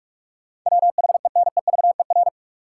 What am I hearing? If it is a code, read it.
Morse code, heard as a keyed tone, about 45 wpm, 700 Hz. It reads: WHENEVER
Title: Sentences from Top 1200 Words - New Word Drill - 45wpm